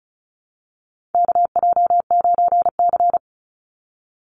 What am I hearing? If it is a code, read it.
Morse code, heard as a keyed tone, about 34 wpm, 700 Hz. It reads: KJ9C
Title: Call Signs Repeated 3x - 35wpm